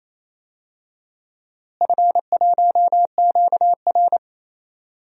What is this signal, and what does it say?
Morse code, heard as a keyed tone, about 28 wpm, 700 Hz. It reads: F1QR
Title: DXCC-100 Call Signs - 4 Characters - Repeated 2x - 28wpm